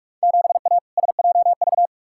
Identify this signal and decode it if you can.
Morse code, heard as a keyed tone, about 46 wpm, 700 Hz. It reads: 7A SJV
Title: Field Day Exchanges Encore - 45wpm